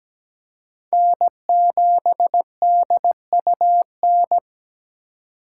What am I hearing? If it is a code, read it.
Morse code, heard as a keyed tone, about 20 wpm, 700 Hz. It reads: N7DUN
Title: Speed Racing - USA 1x3 Call Signs 17wpm